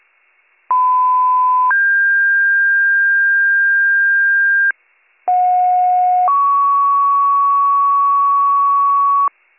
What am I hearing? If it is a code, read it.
Two-tone page, heard as a one-second tone, then a three-second tone, about 1010 then 1600 Hz, about 730 then 1120 Hz. It is heard from 0.7 to 9.3 s.